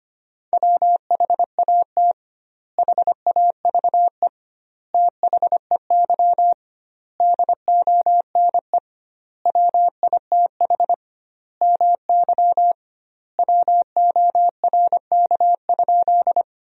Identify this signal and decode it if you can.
Morse code, heard as a keyed tone, about 25 wpm, 700 Hz. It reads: WHAT HAVE THEY DONE WITH MY WORK?
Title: 2x Word-Spacing - Sentences from Top 500 Words - 25wpm